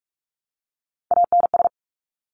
Morse code, heard as a keyed tone, about 46 wpm, 700 Hz. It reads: ANS
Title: Top Three Letter Combinations in Words 45wpm